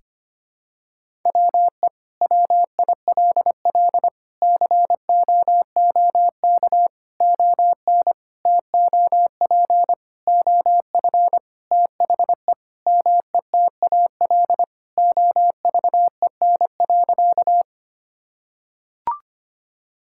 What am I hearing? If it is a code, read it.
Morse code, heard as a keyed tone, about 25 wpm, 700 Hz. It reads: WE WILL COOK ON TOP OF THE METAL OVEN.
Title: Sentences from Top 1000 Words - Review Part 6 - 25wpm